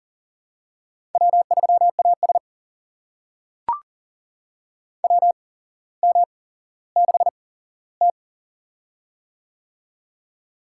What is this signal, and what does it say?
Morse code, heard as a keyed tone, about 42 wpm, spaced as for 17 wpm, 700 Hz. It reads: W3AS WM6T
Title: USA Call Signs - 4 Characters - 8x Character Spacing - 40wpm